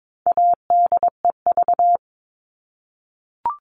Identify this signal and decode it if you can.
Morse code, heard as a keyed tone, about 22 wpm, 700 Hz. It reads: ADEV
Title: All Cut Numbers - 4-Digit 22wpm